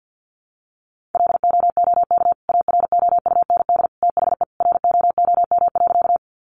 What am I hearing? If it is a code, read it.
Morse code, heard as a keyed tone, about 50 wpm, 700 Hz. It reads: LOOK AROUND THE ROOM.
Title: Sentences from Top 600 Words 50wpm